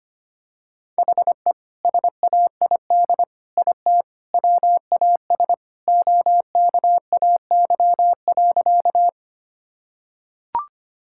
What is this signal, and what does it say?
Morse code, heard as a keyed tone, about 25 wpm, 700 Hz. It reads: HE SAID IT WAS OKAY.
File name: practice / 25wpm / Sentences with 2-Word Phrases - 25wpm